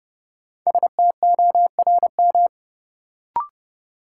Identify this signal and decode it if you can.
Morse code, heard as a keyed tone, about 30 wpm, 700 Hz. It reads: STORM
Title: Sentences from Top 1500 Words - New Word Drill Alt - 30wpm